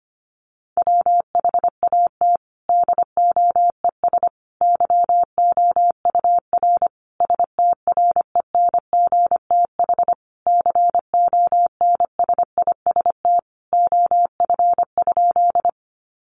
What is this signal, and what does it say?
Morse code, heard as a keyed tone, about 25 wpm, 700 Hz. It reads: WHAT DOES YOUR STRENGTH CONSIST OF?
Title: Sentences from Top 1400 Words 25wpm